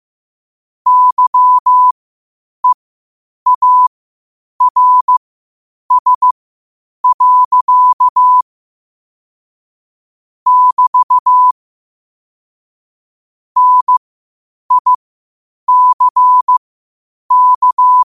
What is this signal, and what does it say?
Morse code, heard as a keyed tone, about 15 wpm, spaced as for 8 wpm, 1000 Hz. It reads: YEARS. <BT> NICK
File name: qotdfiles / 8wpm